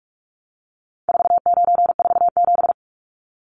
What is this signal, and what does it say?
Morse code, heard as a keyed tone, about 44 wpm, 700 Hz. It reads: 4947
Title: Numbers - 4-Digit 45wpm